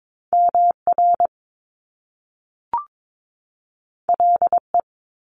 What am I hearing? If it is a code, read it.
Morse code, heard as a keyed tone, about 22 wpm, 700 Hz. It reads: MR LE
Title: Top Two Letter Combinations in Words 22wpm